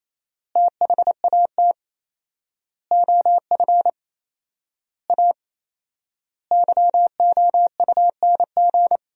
Morse code, heard as a keyed tone, about 28 wpm, 700 Hz. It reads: THAT OF A YOUNG
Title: 4x Word-Spacing - Sentences from Top 500 Words - 28wpm